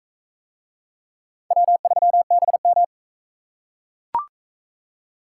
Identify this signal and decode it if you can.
Morse code, heard as a keyed tone, about 42 wpm, 700 Hz. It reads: W3BM